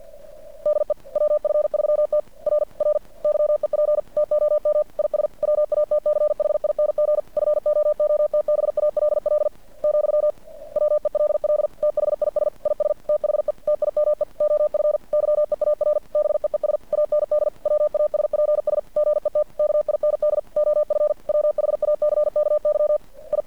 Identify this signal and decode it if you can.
Morse code, heard as a keyed tone, about 50 wpm, 600 Hz. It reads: DE WF3T R R YEP TOM IS WATCHING FOOTBALL , WELL THIS IS THE TIME OF YEAR BEES AND WASPS GET KIND OF WHACKY I
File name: archived / CRSnet